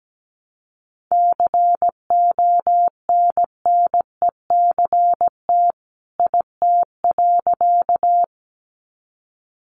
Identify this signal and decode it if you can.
Morse code, heard as a keyed tone, about 17 wpm, 700 Hz. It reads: CONNECT IT.
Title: Sentences from Top 1500 Words 17wpm